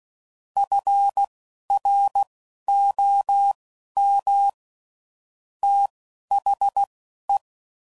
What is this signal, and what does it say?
Morse code, heard as a keyed tone, about 16 wpm, spaced as for 11 wpm, 800 Hz. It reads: FROM THE
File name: CW Code Groups @10WPM / Weather Sets / Weather 4 28